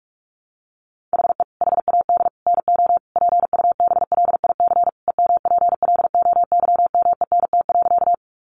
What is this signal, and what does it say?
Morse code, heard as a keyed tone, about 44 wpm, 700 Hz. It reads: HE HAD NO PUBLIC EMPLOYMENT.
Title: Sentences from Top 1500 Words 45wpm